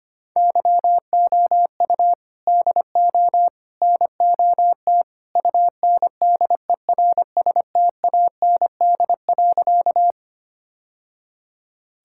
Morse code, heard as a keyed tone, about 25 wpm, 700 Hz. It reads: YOU DO NOT UNDERSTAND.
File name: practice / 25wpm / Sentences with 3-Word Phrases - 25wpm